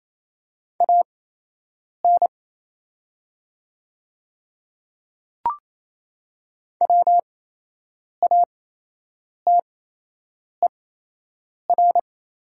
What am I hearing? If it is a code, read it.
Morse code, heard as a keyed tone, about 27 wpm, spaced as for 8 wpm, 700 Hz. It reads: AN WATER
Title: 8x Character Spacing - Top 100 Words - 28wpm